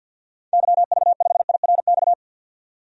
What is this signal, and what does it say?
Morse code, heard as a keyed tone, about 46 wpm, 700 Hz. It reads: YV5IRX
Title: DXCC-100 Call Signs - 6 Characters - 50wpm